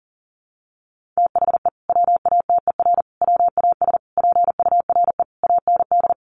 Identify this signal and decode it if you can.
Morse code, heard as a keyed tone, about 40 wpm, 700 Hz. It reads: THE WATER WAS PURE AND
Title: Sentences from Top 1500 Words 40wpm